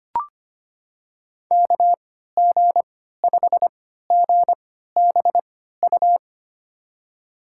Morse code, heard as a keyed tone, about 25 wpm, spaced as for 14 wpm, 700 Hz. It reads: KG5GBU
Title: USA Call Signs - 6 Characters - 3x Character Spacing - 25wpm